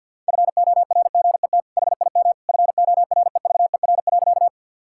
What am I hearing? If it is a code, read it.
Morse code, heard as a keyed tone, about 50 wpm, 700 Hz. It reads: FORGET HIM FOREVER.
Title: Sentences from Top 1600 Words 50wpm